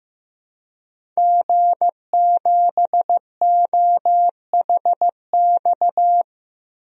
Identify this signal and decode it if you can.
Morse code, heard as a keyed tone, about 16 wpm, 700 Hz. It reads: G7OHX
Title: DXCC-100 Call Signs - 5 Characters - 15wpm